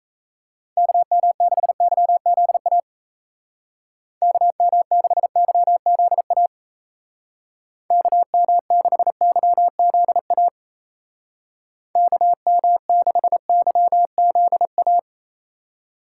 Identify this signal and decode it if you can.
Morse code, heard as a keyed tone, about 33 wpm, 700 Hz. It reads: KM6YZA KM6YZA KM6YZA KM6YZA
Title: Speed Racing - USA Call Signs 28wpm